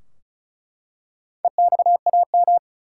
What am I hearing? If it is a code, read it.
Morse code, heard as a keyed tone, about 34 wpm, 700 Hz. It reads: EXAM
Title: Four Letter Words - 35wpm